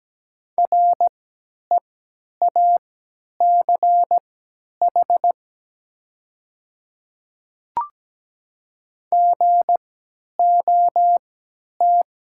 Morse code, heard as a keyed tone, about 17 wpm, spaced as for 10 wpm, 700 Hz. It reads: REACH GOT
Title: 3x Character Spacing - Top 500 Words - 17wpm